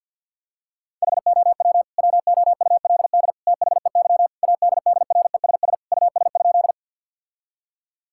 Morse code, heard as a keyed tone, about 47 wpm, 700 Hz. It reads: HOW WOULD THEY ADDRESS US?
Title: Sentences from Top 1000 Words - Review Part 6 - 50wpm